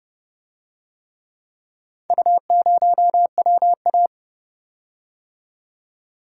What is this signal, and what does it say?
Morse code, heard as a keyed tone, about 29 wpm, 700 Hz. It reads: U0WA